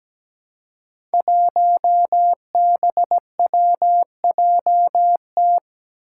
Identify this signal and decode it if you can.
Morse code, heard as a keyed tone, about 17 wpm, 700 Hz. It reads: 1BWJT
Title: Five Challenging Letter-Number Combinations 17wpm